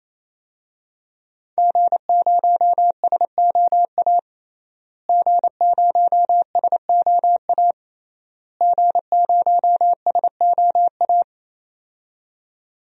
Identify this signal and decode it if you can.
Morse code, heard as a keyed tone, about 28 wpm, 700 Hz. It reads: G0SOA G0SOA G0SOA
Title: DXCC-100 Call Signs - 5 Characters - Repeated 3x - 28wpm